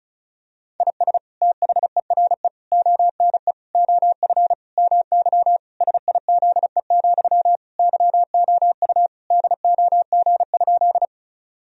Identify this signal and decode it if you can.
Morse code, heard as a keyed tone, about 36 wpm, 700 Hz. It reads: IS THERE ONE OF MY SIZE, YOU DOG?
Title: Sentences from Top 1000 Words - Review Part 2 - 35wpm